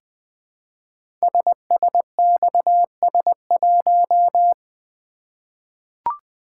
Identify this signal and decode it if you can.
Morse code, heard as a keyed tone, about 20 wpm, 700 Hz. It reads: SSXS1
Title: Code Groups - 5 Letter:Numbers - 20wpm